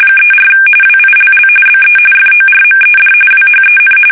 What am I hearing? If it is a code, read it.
RTTY, about 45.45 baud. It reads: 34 WILL BE FORMED WHEN
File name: fsk850 / battleshipnj